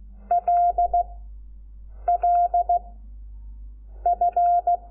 Morse code, heard as a keyed tone, about 16 wpm, 700 Hz. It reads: L L F
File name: Lecciones / CLASE Nº5